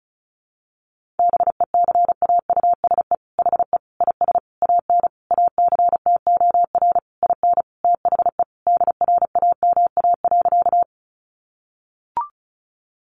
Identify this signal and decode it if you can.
Morse code, heard as a keyed tone, about 34 wpm, 700 Hz. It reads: BECAUSE HE IS AN ACTOR IN THE DRAMA.
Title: Sentences from Top 1500 Words 35wpm